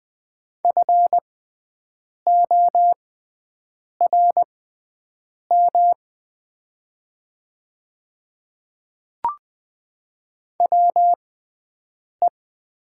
Morse code, heard as a keyed tone, about 20 wpm, spaced as for 7 wpm, 700 Hz. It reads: FORM WE
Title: 6x Character Spacing - Top 200 Words - 20wpm